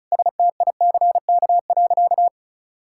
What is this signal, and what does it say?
Morse code, heard as a keyed tone, about 36 wpm, 700 Hz. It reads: STICK.
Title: Sentences from Top 1600 Words 35wpm